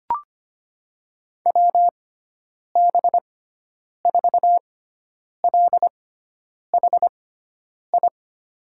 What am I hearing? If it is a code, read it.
Morse code, heard as a keyed tone, about 25 wpm, spaced as for 9 wpm, 700 Hz. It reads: WB4LHI
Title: USA Call Signs - 6 Characters - 6x Character Spacing - 25wpm